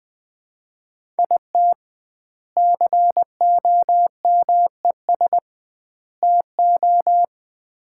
Morse code, heard as a keyed tone, about 20 wpm, 700 Hz. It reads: IT COMES TO
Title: 2x Word-Spacing - Sets of 3 Words - 20wpm